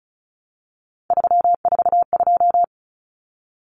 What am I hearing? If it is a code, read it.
Morse code, heard as a keyed tone, about 34 wpm, 700 Hz. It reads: 342
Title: Numbers - 3-Digit 35wpm